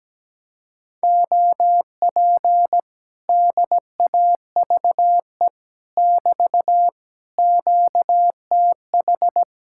Morse code, heard as a keyed tone, about 17 wpm, 700 Hz. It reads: OP DAVE <BT> QTH
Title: Speed-Racing Common QSO Phrases Encore - 17wpm